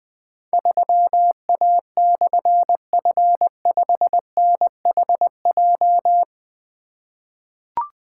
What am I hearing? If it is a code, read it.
Morse code, heard as a keyed tone, about 20 wpm, 700 Hz. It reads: 3A/F5NHJ
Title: Difficult Call Signs 20wpm